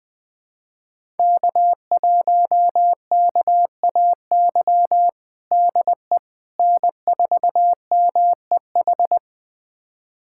Morse code, heard as a keyed tone, about 20 wpm, 700 Hz. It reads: K1KAY DE N4MEH
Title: USA Callsign DE Callsign - 5 Characters - 20wpm